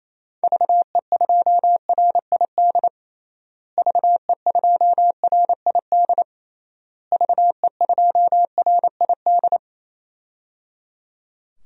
Morse code, heard as a keyed tone, about 28 wpm, 700 Hz. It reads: VE2RID VE2RID VE2RID
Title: DXCC-100 Call Signs - 6 Characters - Repeated 3x - 28wpm